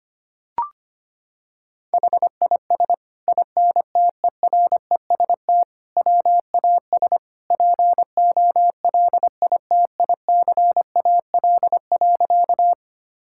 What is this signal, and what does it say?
Morse code, heard as a keyed tone, about 25 wpm, 700 Hz. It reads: HIS INTEREST WAS POLITICAL.
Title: Sentences from Top 1000 Words - Review Part 5 - 25wpm